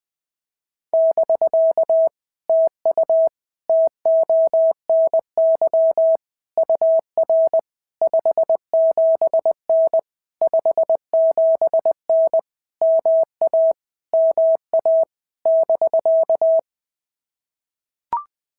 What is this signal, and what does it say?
Morse code, heard as a keyed tone, about 20 wpm, 650 Hz. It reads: <BK> TU TONY UR 57N 57N MA MA <BK>
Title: POTA QSO Practice - 20wpm